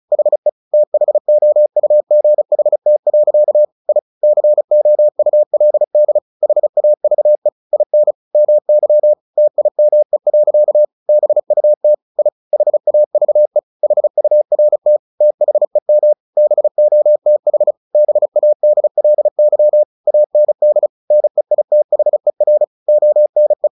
Morse code, heard as a keyed tone, about 34 wpm, 600 Hz. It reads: HE THOUGHT. I COULD HAVE IN MY TIME. BUT I HAVE HURT THEM BOTH BADLY AND NEITHER ONE
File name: the old man and the sea 2 35wpm